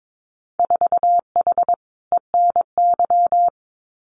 Morse code, heard as a keyed tone, about 22 wpm, 700 Hz. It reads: 4H ENY
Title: Winter Field Day Exchanges Encore 22wpm